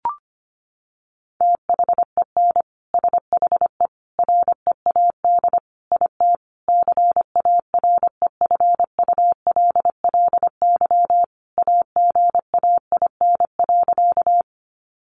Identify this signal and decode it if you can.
Morse code, heard as a keyed tone, about 25 wpm, 700 Hz. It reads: THEN SHE READ IT CAREFULLY AGAIN.
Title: Sentences from Top 1300 Words 25wpm